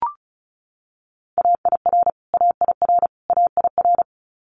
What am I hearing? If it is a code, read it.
Morse code, heard as a keyed tone, about 36 wpm, 700 Hz. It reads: AIR AIR AIR
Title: Multi-letter Phonemes Repeated 3x 35wpm